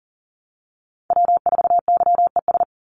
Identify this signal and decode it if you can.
Morse code, heard as a keyed tone, about 40 wpm, 700 Hz. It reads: W4YES
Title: Call Signs Repeated 3x - 40wpm